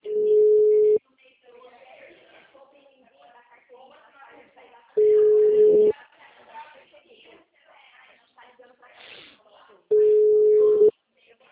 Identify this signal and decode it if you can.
Ringback tone (Europe, 425 Hz), with speech.
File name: out- -2011- - - .3477